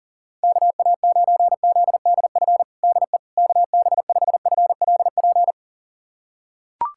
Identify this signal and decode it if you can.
Morse code, heard as a keyed tone, about 38 wpm, 700 Hz. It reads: KA9ZDF DE KB5FLP
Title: USA Callsign DE Callsign - 6 Characters - 40wpm